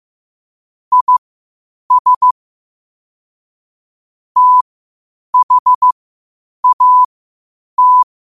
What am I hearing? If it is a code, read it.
Morse code, heard as a keyed tone, about 15 wpm, spaced as for 8 wpm, 1000 Hz. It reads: IS THAT
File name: qotdfiles / 8wpm